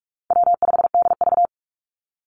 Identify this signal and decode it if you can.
Morse code, heard as a keyed tone, about 44 wpm, 700 Hz. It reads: W5DV
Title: Call Signs - 45wpm